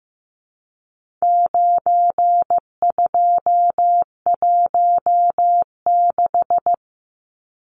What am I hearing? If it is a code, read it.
Morse code, heard as a keyed tone, about 15 wpm, 700 Hz. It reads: 9216
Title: Numbers - 4-Digit 15wpm